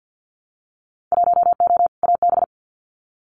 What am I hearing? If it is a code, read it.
Morse code, heard as a keyed tone, about 50 wpm, 700 Hz. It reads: 1O AB
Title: Winter Field Day Exchanges Encore 50wpm